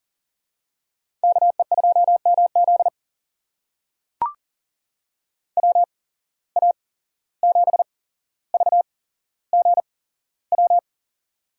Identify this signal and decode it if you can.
Morse code, heard as a keyed tone, about 38 wpm, spaced as for 16 wpm, 700 Hz. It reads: KE2MZ WA7VGW
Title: USA Call Signs - 8x Character Spacing - 40wpm